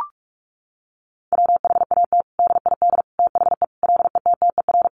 Morse code, heard as a keyed tone, about 44 wpm, 700 Hz. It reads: WHAT DID THE LETTER
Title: Sentences from Top 500 Words 45wpm